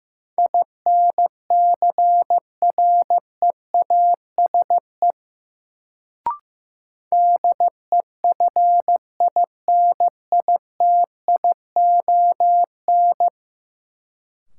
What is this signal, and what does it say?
Morse code, heard as a keyed tone, about 15 wpm, 700 Hz. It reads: INCREASE DEFINITION
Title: Sentences from Top 1000 Words - Review Part 4 - New Word Drill - 15wpm